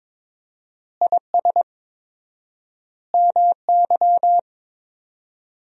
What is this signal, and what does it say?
Morse code, heard as a keyed tone, about 22 wpm, 700 Hz. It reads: IS MY
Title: 4x Word-Spacing - Sentences from Top 200 Words - 22wpm